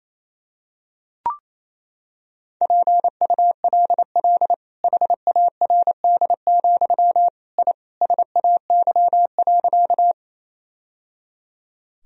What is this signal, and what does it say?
Morse code, heard as a keyed tone, about 28 wpm, 700 Hz. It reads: PULL HARD, I SAY.
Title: Sentences from Top 1000 Words - Review Part 2 - 28wpm